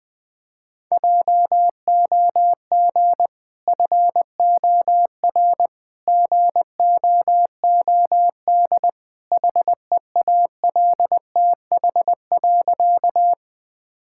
Morse code, heard as a keyed tone, about 20 wpm, 700 Hz. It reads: JOG FOR GOOD HEALTH.
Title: Easy Sentences with Hints - 20wpm